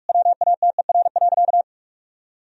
Morse code, heard as a keyed tone, about 46 wpm, 700 Hz. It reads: WATER.
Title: Sentences from Top 1000 Words - Review Part 3 - 45wpm